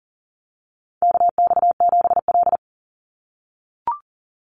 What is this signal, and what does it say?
Morse code, heard as a keyed tone, about 40 wpm, 700 Hz. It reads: KX7L
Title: NA SOTA Chaser Callsigns 40wpm